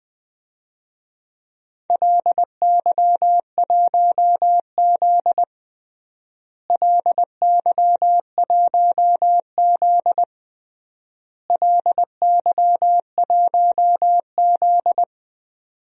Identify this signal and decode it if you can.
Morse code, heard as a keyed tone, about 21 wpm, 700 Hz. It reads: LY1Z LY1Z LY1Z
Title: DXCC-100 Call Signs - 4 Characters - Repeated 3x - 20wpm